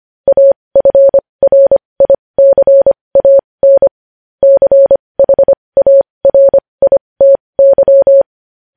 Morse code, heard as a keyed tone, about 25 wpm, spaced as for 20 wpm, 550 Hz. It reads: AFRICAN CHARITY
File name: podcast / News headlines-1x-25x20